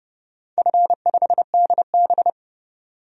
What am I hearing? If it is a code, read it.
Morse code, heard as a keyed tone, about 30 wpm, 700 Hz. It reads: F5DB
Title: DXCC-100 Call Signs - 4 Characters - 30wpm